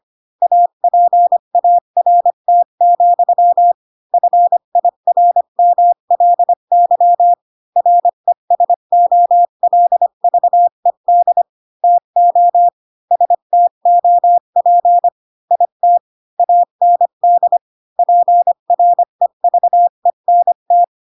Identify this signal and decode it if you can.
Morse code, heard as a keyed tone, about 25 wpm, 700 Hz. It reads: APART, FIRMLY RESOLVED TO STOP IT AND PREVENT
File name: Pinocchio / Pinocchio ch3 20 25 ik1hge